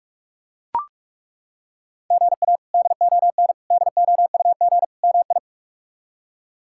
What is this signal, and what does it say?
Morse code, heard as a keyed tone, about 42 wpm, 700 Hz. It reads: GA DON DOUG MI
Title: K1USN SST QSO Practice - 45wpm